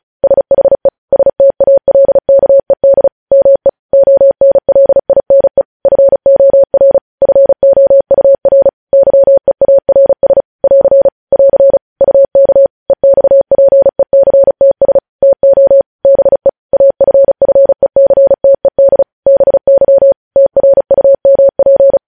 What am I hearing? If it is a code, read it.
Morse code, heard as a keyed tone, about 36 wpm, 550 Hz. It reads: SHE STALKED ME ONLINE FOR FOUR YEARS <AR> <AR> UK EXPECTS TO BE AFFECTED BY TRUMP
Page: News Headlines in Morse Code at 35 WPM